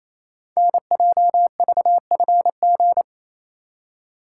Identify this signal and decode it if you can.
Morse code, heard as a keyed tone, about 29 wpm, 700 Hz. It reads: NJVFG
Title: Code Groups - 5 Letters - 28wpm